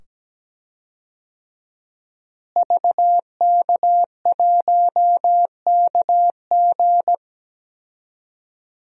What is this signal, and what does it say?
Morse code, heard as a keyed tone, about 17 wpm, 700 Hz. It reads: VK1KG
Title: DXCC-100 Call Signs - 5 Characters - Repeated 2x - 17wpm